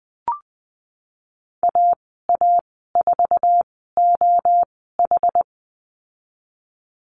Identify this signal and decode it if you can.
Morse code, heard as a keyed tone, about 20 wpm, spaced as for 15 wpm, 700 Hz. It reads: AA4OH
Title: 062 Lesson 11g - Letter h - Call Signs 20wpm